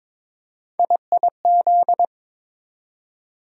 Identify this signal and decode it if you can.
Morse code, heard as a keyed tone, about 22 wpm, 700 Hz. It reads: IIZ
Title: USA Call Sign Postfixes - 3 Letters 22wpm